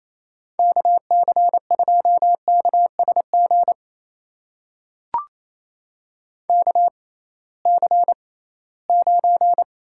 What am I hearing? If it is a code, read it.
Morse code, heard as a keyed tone, about 28 wpm, spaced as for 18 wpm, 700 Hz. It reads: KC2KSG KC9
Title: USA Call Signs - 6 Characters - 6x Character Spacing - 28wpm